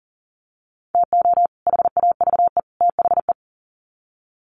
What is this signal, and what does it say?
Morse code, heard as a keyed tone, about 40 wpm, 700 Hz. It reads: TO HAVE THE
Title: Sets of 3 Words 40wpm